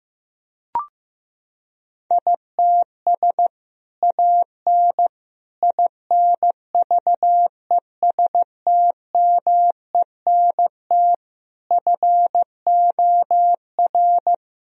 Morse code, heard as a keyed tone, about 15 wpm, 700 Hz. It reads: ITS AN INVESTMENT FOR
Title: Sentences from Top 1000 Words - Review Part 4 - 15wpm